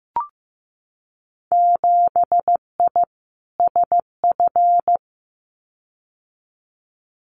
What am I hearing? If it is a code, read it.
Morse code, heard as a keyed tone, about 15 wpm, 700 Hz. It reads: 7I SF
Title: Winter Field Day Exchanges Encore 15wpm